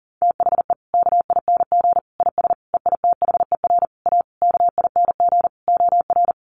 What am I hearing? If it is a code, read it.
Morse code, heard as a keyed tone, about 40 wpm, 700 Hz. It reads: THE KING IS EITHER A KING OR
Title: Sentences from Top 600 Words 40wpm